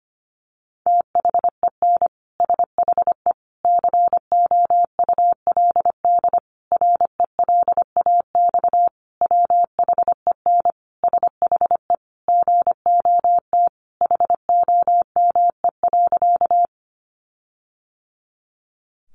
Morse code, heard as a keyed tone, about 25 wpm, 700 Hz. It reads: THEN SHE COULD RELAX WHEN SHE GOT HOME.
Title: Sentences from Top 1400 Words 25wpm